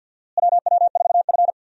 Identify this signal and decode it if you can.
Morse code, heard as a keyed tone, about 52 wpm, 700 Hz. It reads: WW4F
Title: USA Call Signs - 4 Characters - Repeated 3x - 50wpm